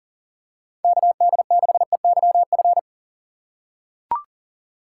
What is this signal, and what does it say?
Morse code, heard as a keyed tone, about 40 wpm, 700 Hz. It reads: KD6EYF
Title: USA Call Signs - 6 Characters - 2x Character Spacing - 40wpm